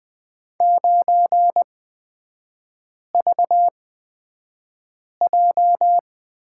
Morse code, heard as a keyed tone, about 20 wpm, spaced as for 5 wpm, 700 Hz. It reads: 9VJ
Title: 236 Lesson 38e - Letter q - Three Characters 20wpm